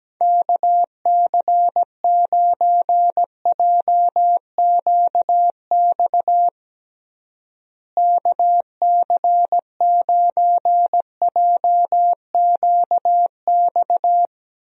Morse code, heard as a keyed tone, about 17 wpm, 700 Hz. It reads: KC9JQX KC9JQX
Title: USA Call Signs - 6 Characters - Repeated 3x - 17wpm